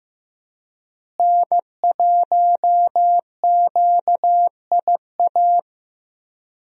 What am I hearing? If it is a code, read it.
Morse code, heard as a keyed tone, about 15 wpm, 700 Hz. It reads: N1QIA
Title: USA Call Signs - 5 Characters - Repeated 3x - 15wpm